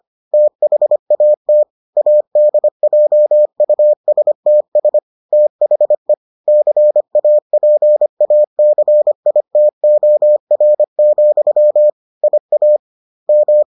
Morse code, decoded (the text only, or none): THAT ADJUSTS THE CAPACITOR, IAM